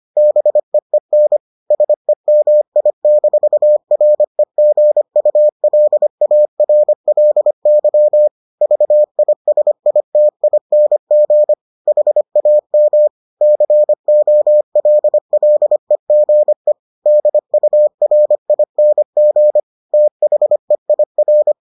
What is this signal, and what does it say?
Morse code, heard as a keyed tone, about 25 wpm, 600 Hz. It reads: BEEN SEMI-REGULARLY VISITING HAM COLLEGE DURING THEIR